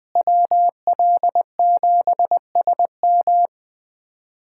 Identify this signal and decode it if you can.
Morse code, heard as a keyed tone, about 21 wpm, 700 Hz. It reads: WL7SM
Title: USA Callsign DE Callsign - 5 Characters - 20wpm